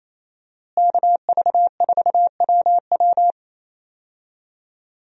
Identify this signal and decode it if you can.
Morse code, heard as a keyed tone, about 28 wpm, 700 Hz. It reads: KV4WW
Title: USA Call Signs - 5 Characters - Repeated 2x - 28wpm